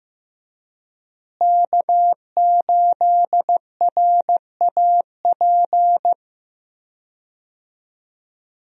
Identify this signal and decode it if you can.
Morse code, heard as a keyed tone, about 15 wpm, 700 Hz. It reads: K8RAP
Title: USA Call Signs - 5 Characters - Repeated 3x - 15wpm